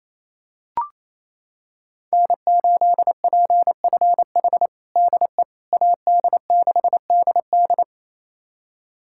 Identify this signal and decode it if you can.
Morse code, heard as a keyed tone, about 28 wpm, 700 Hz. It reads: N8PFH DE AD6DD
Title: USA Callsign DE Callsign - 5 Characters - 28wpm